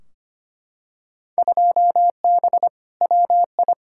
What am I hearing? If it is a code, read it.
Morse code, heard as a keyed tone, about 25 wpm, 700 Hz. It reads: 2B WI
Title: Field Day Exchanges - 25wpm